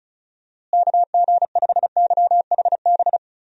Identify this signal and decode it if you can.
Morse code, heard as a keyed tone, about 36 wpm, 700 Hz. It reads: KG5YHB
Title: USA Call Signs - 6 Characters - Repeated 2x - 35wpm